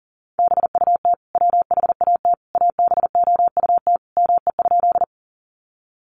Morse code, heard as a keyed tone, about 40 wpm, 700 Hz. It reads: BUT WHAT ABOUT ME?
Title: Sentences from Top 200 Words 40wpm